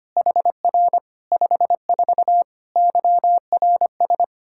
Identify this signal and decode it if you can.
Morse code, heard as a keyed tone, about 25 wpm, 700 Hz. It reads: HR 54 YRS
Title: Practice QSO - Exchange 3 - More Info - 25wpm